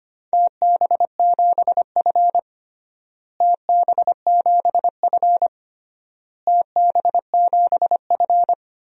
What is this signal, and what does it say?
Morse code, heard as a keyed tone, about 25 wpm, 700 Hz. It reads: TB7F TB7F TB7F
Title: DXCC-100 Call Signs - 4 Characters - Repeated 3x - 25wpm